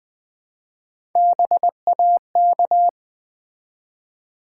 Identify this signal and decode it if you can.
Morse code, heard as a keyed tone, about 20 wpm, 700 Hz. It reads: BAK
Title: Build A Word - 1 Letter At A Time - 4-Letter Words - 20wpm